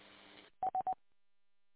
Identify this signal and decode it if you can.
There is no code.